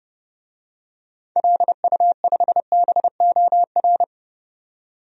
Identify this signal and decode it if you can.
Morse code, heard as a keyed tone, about 30 wpm, 700 Hz. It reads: LU5BOR
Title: DXCC-100 Call Signs - 6 Characters - 30wpm